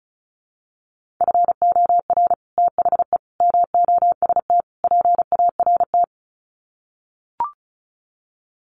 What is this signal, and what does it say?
Morse code, heard as a keyed tone, about 34 wpm, 700 Hz. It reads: FOR THE MOST PART
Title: Sets of 4 Words 35wpm